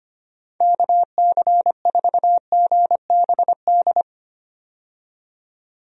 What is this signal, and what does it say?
Morse code, heard as a keyed tone, about 25 wpm, 700 Hz. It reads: KC4GBD
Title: USA Call Signs - 6 Characters - 25wpm